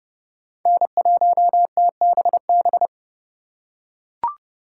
Morse code, heard as a keyed tone, about 30 wpm, 700 Hz. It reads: N1TBB
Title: USA Call Signs - 4x Character Spacing - 30wpm